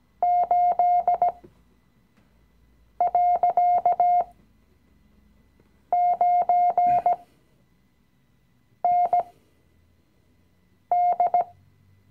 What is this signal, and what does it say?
Morse code, heard as a keyed tone, about 17 wpm, spaced as for 5 wpm, 710 Hz. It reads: 8.9ND
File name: CW-Online-Kurs / Anfänger / Lektion 017